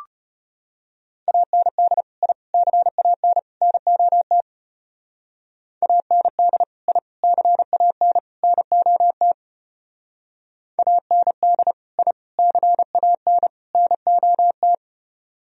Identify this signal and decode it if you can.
Morse code, heard as a keyed tone, about 32 wpm, 700 Hz. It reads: AND I CAN NOT AND I CAN NOT AND I CAN NOT